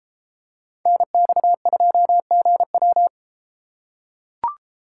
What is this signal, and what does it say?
Morse code, heard as a keyed tone, about 32 wpm, 700 Hz. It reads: NX2GW